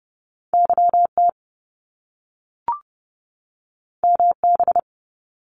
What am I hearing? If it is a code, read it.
Morse code, heard as a keyed tone, about 30 wpm, 700 Hz. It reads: YT MB
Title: Canadian Provinces and Territories 30wpm